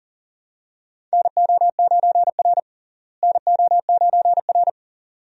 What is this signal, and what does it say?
Morse code, heard as a keyed tone, about 40 wpm, 700 Hz. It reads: NO9R NO9R
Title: USA Call Signs - 4 Characters - Repeated 2x - 40wpm